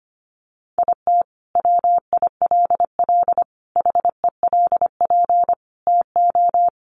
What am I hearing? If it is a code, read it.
Morse code, heard as a keyed tone, about 25 wpm, 700 Hz. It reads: IT WILL HELP TO
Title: Sentences from Top 1400 Words 25wpm